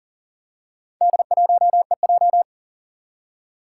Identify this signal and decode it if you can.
Morse code, heard as a keyed tone, about 38 wpm, 700 Hz. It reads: D1EJ
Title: Four Challenging Letter-Number Combinations 40wpm